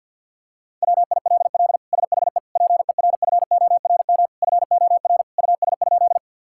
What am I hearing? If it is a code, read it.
Morse code, heard as a keyed tone, about 50 wpm, 700 Hz. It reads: WILL SHE PERFORM FOR US?
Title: Sentences from Top 1000 Words - Review Part 5 - 50wpm